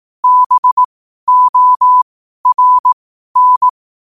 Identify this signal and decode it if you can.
Morse code, heard as a keyed tone, about 18 wpm, spaced as for 12 wpm, 1000 Hz. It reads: BORN
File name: qotdfiles / 12wpm